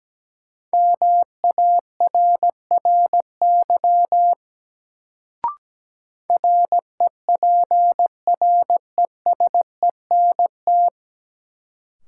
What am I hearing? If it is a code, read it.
Morse code, heard as a keyed tone, about 17 wpm, 700 Hz. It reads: MARRY REPRESENT
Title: Sentences from Top 1200 Words - New Word Drill Alt - 17wpm